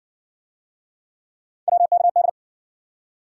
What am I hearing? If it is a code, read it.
Morse code, heard as a keyed tone, about 60 wpm, 700 Hz. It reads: WKD